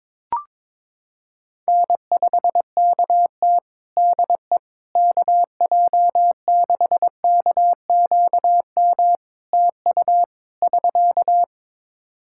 Morse code, heard as a keyed tone, about 22 wpm, 700 Hz. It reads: N5KT DE KJ6KQM TU <SK>
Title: Practice QSO - Exchange 4 - Ending - 22wpm